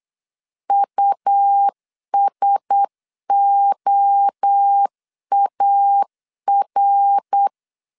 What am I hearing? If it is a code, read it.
Morse code, heard as a keyed tone, about 8 wpm, 800 Hz. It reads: USOAR
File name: LEZIONI / 7 GF9Z / 7 GF 8wpm 3108